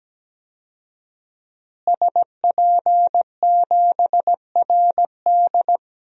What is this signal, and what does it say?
Morse code, heard as a keyed tone, about 17 wpm, 700 Hz. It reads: SP7RD
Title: DXCC-100 Call Signs - 5 Characters - Repeated 2x - 17wpm